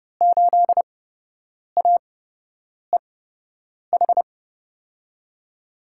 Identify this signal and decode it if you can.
Morse code, heard as a keyed tone, about 29 wpm, spaced as for 8 wpm, 700 Hz. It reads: 8AEH
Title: USA Call Signs - 5 Characters - 8x Character Spacing - 30wpm